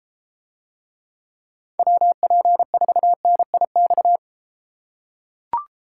Morse code, heard as a keyed tone, about 33 wpm, 700 Hz. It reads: WP4NIX